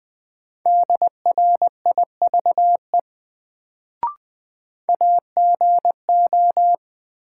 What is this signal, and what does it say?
Morse code, heard as a keyed tone, about 20 wpm, 700 Hz. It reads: DRIVE AGO
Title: Sentences from Top 500 Words - New Word Drill - 20wpm